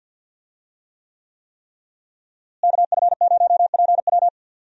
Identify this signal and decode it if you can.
Morse code, heard as a keyed tone, about 52 wpm, 700 Hz. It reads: KF0PW
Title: USA Call Signs - 5 Characters - Repeated 3x - 50wpm